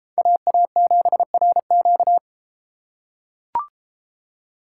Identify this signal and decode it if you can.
Morse code, heard as a keyed tone, about 33 wpm, 700 Hz. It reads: AA7RQ